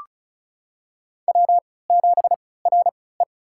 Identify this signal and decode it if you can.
Morse code, heard as a keyed tone, about 36 wpm, spaced as for 20 wpm, 700 Hz. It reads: W7RE
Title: USA Call Signs - 4 Characters - 3x Character Spacing - 35wpm